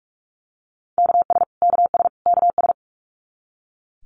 Morse code, heard as a keyed tone, about 44 wpm, 700 Hz. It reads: KS KS KS
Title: US States Repeated 3x 45wpm